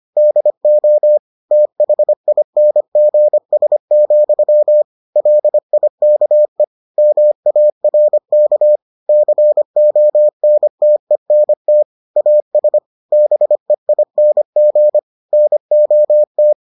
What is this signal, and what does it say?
Morse code, heard as a keyed tone, about 25 wpm, 600 Hz. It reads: DO THINGS, LIKE MARK CONTENT AS BEING NOT